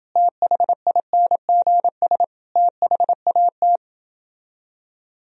Morse code, heard as a keyed tone, about 30 wpm, 700 Hz. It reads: THINGS THAT